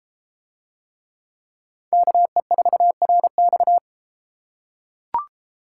Morse code, heard as a keyed tone, about 32 wpm, 700 Hz. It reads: KE4RX